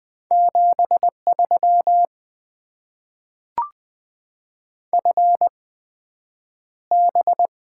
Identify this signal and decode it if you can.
Morse code, heard as a keyed tone, about 20 wpm, spaced as for 9 wpm, 700 Hz. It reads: 73 FB
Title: Top 20 QSO Elements - 8x Character Spacing - 20wpm